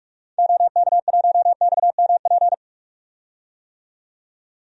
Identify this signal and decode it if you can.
Morse code, heard as a keyed tone, about 46 wpm, 700 Hz. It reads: OK1XMP
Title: DXCC-100 Call Signs - 6 Characters - 45wpm